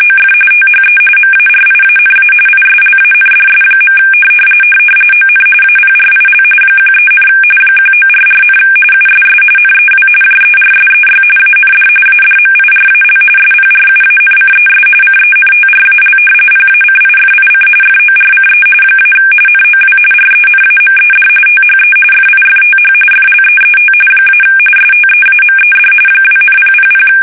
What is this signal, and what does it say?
RTTY, about 45.45 baud. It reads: OF THE MARIANA ISLANDS. ON APRIL 13, 1944, USS NEW JERSEY AND ADMIRAL SPRUANCES CENTRAL PACIFIC FORCE - NOW RENAMED THE FIFTH FLEET - STEAMED 2,000
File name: fsk850 / battleshipnj